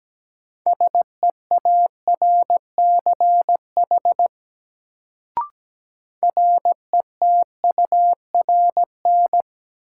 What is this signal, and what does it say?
Morse code, heard as a keyed tone, about 17 wpm, 700 Hz. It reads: SEARCH RETURN
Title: Sentences from Top 800 Words - New Word Drill - 17wpm